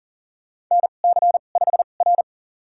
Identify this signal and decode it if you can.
Morse code, heard as a keyed tone, about 43 wpm, spaced as for 30 wpm, 700 Hz. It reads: NC5R
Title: USA Call Signs - 4 Characters - 2x Character Spacing - 40wpm